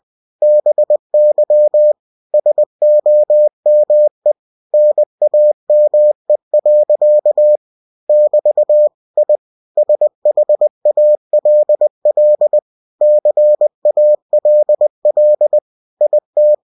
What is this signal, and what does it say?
Morse code, heard as a keyed tone, about 20 wpm, 600 Hz. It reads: BY SOME NAME. <BT> I SHALL CALL IT
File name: cw-books / winnie-the-pooh / winnie-20wpm-page-0074